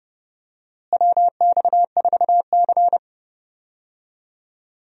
Morse code, heard as a keyed tone, about 30 wpm, 700 Hz. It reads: WX4C
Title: USA Call Signs - 4 Characters - Repeated 2x - 30wpm